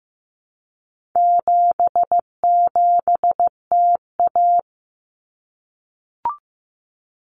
Morse code, heard as a keyed tone, about 15 wpm, 700 Hz. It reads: 77TA
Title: Common Cut Numbers - 4-Digit 15wpm